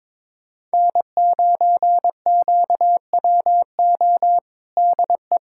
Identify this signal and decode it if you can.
Morse code, heard as a keyed tone, about 22 wpm, 700 Hz. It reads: N9QWO DE
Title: USA Callsign DE Callsign - 5 Characters - 22wpm